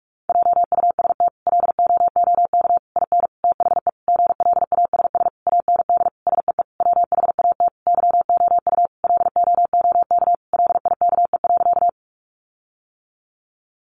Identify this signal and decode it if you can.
Morse code, heard as a keyed tone, about 44 wpm, 700 Hz. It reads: JUST LOOK IN THE GLASS AND SEE WHAT YOU LOOK LIKE.
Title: Sentences from Top 1300 Words 45wpm